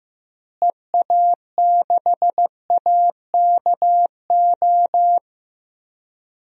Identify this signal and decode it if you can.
Morse code, heard as a keyed tone, about 15 wpm, 700 Hz. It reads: EA6AKO
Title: DXCC-100 Call Signs - 6 Characters - Repeated 3x - 15wpm